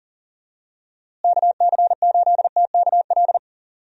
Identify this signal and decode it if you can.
Morse code, heard as a keyed tone, about 40 wpm, 700 Hz. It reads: KC8TKL
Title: USA Call Signs - 6 Characters - Repeated 3x - 40wpm